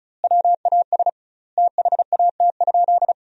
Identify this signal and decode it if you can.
Morse code, heard as a keyed tone, about 34 wpm, 700 Hz. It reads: WAS THAT?
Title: 2x Word-Spacing - Sentences from Top 100 Words - 35wpm